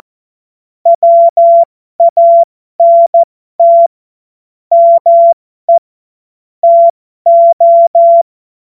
Morse code, heard as a keyed tone, about 14 wpm, spaced as for 12 wpm, 680 Hz. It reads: WANT ME TO
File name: Morsekurs / Witze